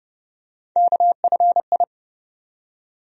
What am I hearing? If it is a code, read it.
Morse code, heard as a keyed tone, about 32 wpm, 700 Hz. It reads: KFI
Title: USA Call Sign Postfixes - 3 Letters 30wpm